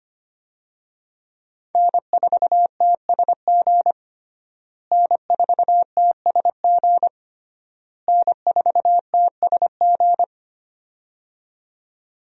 Morse code, heard as a keyed tone, about 25 wpm, 700 Hz. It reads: N4TSG N4TSG N4TSG
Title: USA Call Signs - 5 Characters - Repeated 3x - 25wpm